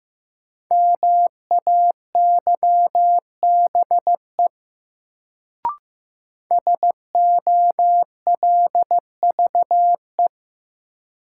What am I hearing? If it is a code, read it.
Morse code, heard as a keyed tone, about 15 wpm, 700 Hz. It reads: MAYBE SOLVE
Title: Sentences from Top 1000 Words - New Word Drill - 15wpm